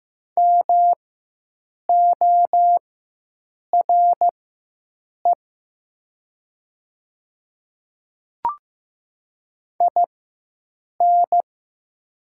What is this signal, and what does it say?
Morse code, heard as a keyed tone, about 15 wpm, spaced as for 7 wpm, 700 Hz. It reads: MORE IN
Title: 4x Character Spacing - Top 200 Words - 15wpm